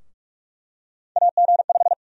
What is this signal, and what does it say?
Morse code, heard as a keyed tone, about 44 wpm, 700 Hz. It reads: AG5
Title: USA Call Sign Prefixes - 45wpm